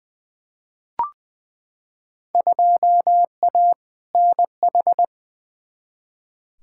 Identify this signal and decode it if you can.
Morse code, heard as a keyed tone, about 20 wpm, 700 Hz. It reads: 2A NH
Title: Field Day Exchanges - 20wpm